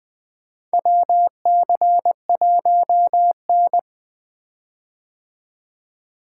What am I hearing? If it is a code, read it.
Morse code, heard as a keyed tone, about 20 wpm, 700 Hz. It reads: WC1N
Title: 2023 POTA Activator Call Signs - 20wpm